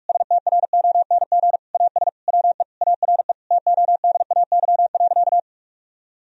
Morse code, heard as a keyed tone, about 41 wpm, 700 Hz. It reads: STRONG AS WE ARE TODAY.
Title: Sentences from Top 1000 Words - Review Part 4 - 45wpm